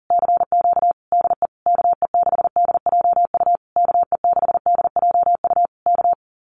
Morse code, heard as a keyed tone, about 40 wpm, 700 Hz. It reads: CQ DE KE6DJU KE6DJU K
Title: Common QSO Phrases Encore 40wpm